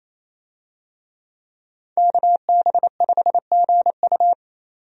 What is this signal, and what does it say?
Morse code, heard as a keyed tone, about 27 wpm, 700 Hz. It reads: KB5GU
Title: USA Call Signs - 5 Characters - 28wpm